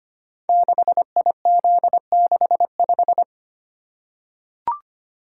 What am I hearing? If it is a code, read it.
Morse code, heard as a keyed tone, about 25 wpm, 700 Hz. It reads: 6IZ65
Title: Code Groups - 5 Letter:Numbers - 25wpm